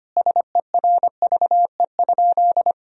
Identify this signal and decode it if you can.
Morse code, heard as a keyed tone, about 25 wpm, 700 Hz. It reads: SERVE?
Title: Sentences from Top 400 Words and Abbreviations 25wpm